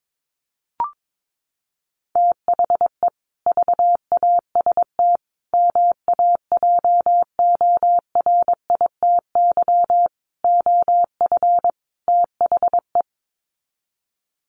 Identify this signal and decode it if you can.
Morse code, heard as a keyed tone, about 22 wpm, 700 Hz. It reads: THE VAST MAJORITY OF THE
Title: Sets of 5 Words Encore 22wpm